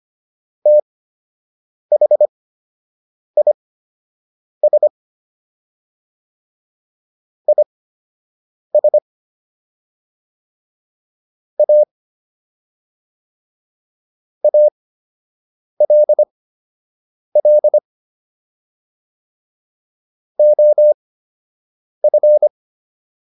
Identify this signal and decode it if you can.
Morse code, heard as a keyed tone, about 25 wpm, spaced as for 7 wpm, 600 Hz. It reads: THIS IS A ALL OF